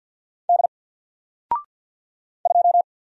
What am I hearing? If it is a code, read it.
Morse code, heard as a keyed tone, about 50 wpm, 700 Hz. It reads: D 2
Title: Single Letter-Number 50wpm